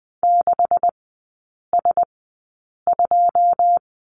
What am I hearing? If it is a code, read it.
Morse code, heard as a keyed tone, about 20 wpm, spaced as for 8 wpm, 690 Hz. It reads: 6S2
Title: 133 Lesson 22e - Question Mark - Three Characters 20wpm